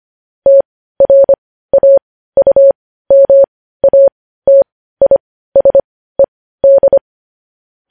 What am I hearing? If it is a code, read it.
Morse code, heard as a keyed tone, about 25 wpm, spaced as for 15 wpm, 550 Hz. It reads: TRAUMATISED